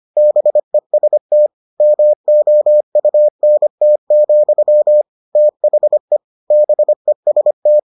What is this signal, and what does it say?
Morse code, heard as a keyed tone, about 25 wpm, 600 Hz. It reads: BEST MOUNT, THE BEST